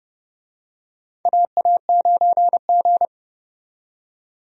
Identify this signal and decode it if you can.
Morse code, heard as a keyed tone, about 29 wpm, 700 Hz. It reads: AA9G
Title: USA Call Signs - 4 Characters - 30wpm